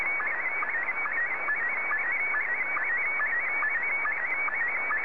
SSTV (slow-scan television), Scottie 1.